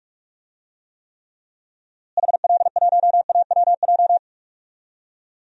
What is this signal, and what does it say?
Morse code, heard as a keyed tone, about 46 wpm, 700 Hz. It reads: HL1AWJ
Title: DXCC-100 Call Signs - 6 Characters - Repeated 2x - 45wpm